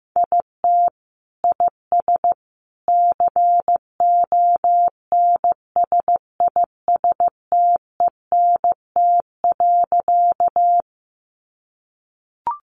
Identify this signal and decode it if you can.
Morse code, heard as a keyed tone, about 15 wpm, 700 Hz. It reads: IT IS CONSISTENT.
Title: Sentences from Top 1500 Words 15wpm